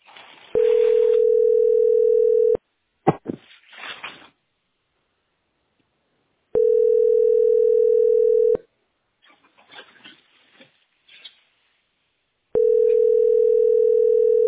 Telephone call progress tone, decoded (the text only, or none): ringback tone